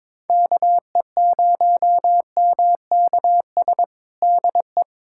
Morse code, heard as a keyed tone, about 21 wpm, 700 Hz. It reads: KE0MKS DE
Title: USA Callsign DE Callsign - 6 Characters - 22wpm